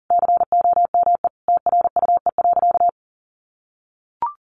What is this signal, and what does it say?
Morse code, heard as a keyed tone, about 37 wpm, 700 Hz. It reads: COME TRUE.
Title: Sentences from Top 1500 Words 40wpm